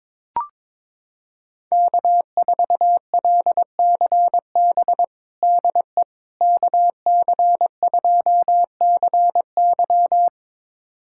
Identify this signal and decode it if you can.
Morse code, heard as a keyed tone, about 22 wpm, 700 Hz. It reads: K4LCB DE KC2CY
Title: USA Callsign DE Callsign - 5 Characters - 22wpm